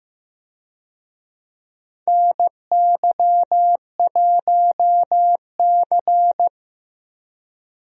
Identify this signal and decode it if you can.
Morse code, heard as a keyed tone, about 15 wpm, 700 Hz. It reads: NY1C
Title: USA Call Signs - 4 Characters - Repeated 2x - 15wpm